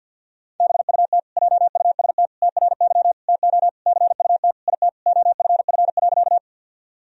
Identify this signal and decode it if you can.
Morse code, heard as a keyed tone, about 46 wpm, 700 Hz. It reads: BUT JUST TRY TO CUT IT OFF.
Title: Sentences from Top 1000 Words - Review Part 1 - 50wpm